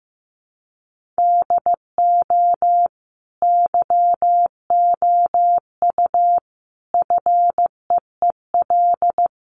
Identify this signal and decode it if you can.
Morse code, heard as a keyed tone, about 14 wpm, 700 Hz. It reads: DO YOU FEEL
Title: Sentences from Top 1400 Words 15wpm